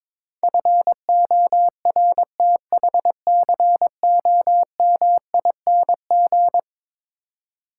Morse code, heard as a keyed tone, about 22 wpm, 700 Hz. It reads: FORTHCOMING
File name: practice / 22wpm / Eleven Letter Words - 22wpm